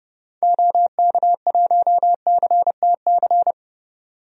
Morse code, heard as a keyed tone, about 30 wpm, 700 Hz. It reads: OK1CTC
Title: DXCC-100 Call Signs - 6 Characters - 30wpm